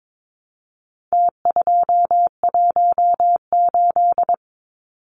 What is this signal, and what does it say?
Morse code, heard as a keyed tone, about 22 wpm, 700 Hz. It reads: T218
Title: Numbers - Contest Style - 4-Digit 22wpm